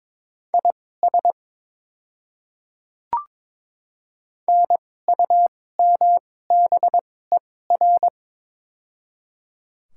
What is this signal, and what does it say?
Morse code, heard as a keyed tone, about 22 wpm, spaced as for 16 wpm, 700 Hz. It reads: IS NUMBER
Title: 2x Character Spacing - Top 100 Words - 22wpm